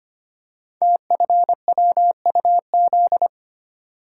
Morse code, heard as a keyed tone, about 25 wpm, 700 Hz. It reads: TFWUZ
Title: Code Groups - 5 Letter:Numbers - 25wpm